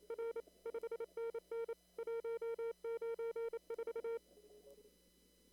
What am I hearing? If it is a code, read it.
Morse code, heard as a keyed tone, about 27 wpm, 460 Hz. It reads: R 5NN 194
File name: CW / kt5j wpx cw-op-k5pi